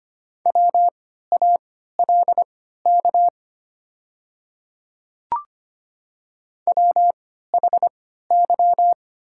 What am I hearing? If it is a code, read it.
Morse code, heard as a keyed tone, about 25 wpm, spaced as for 14 wpm, 700 Hz. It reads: WALK WHY
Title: 3x Character Spacing - Top 300 Words - 25wpm